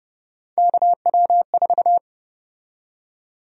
Morse code, heard as a keyed tone, about 30 wpm, 700 Hz. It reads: KW4
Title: USA Call Sign Prefixes - Repeated 2x - 30wpm